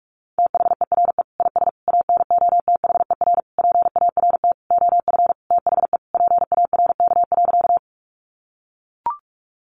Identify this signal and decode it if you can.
Morse code, heard as a keyed tone, about 44 wpm, 700 Hz. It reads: THERE IS ANOTHER PART OF THE PARK.
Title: Sentences from Top 1300 Words 45wpm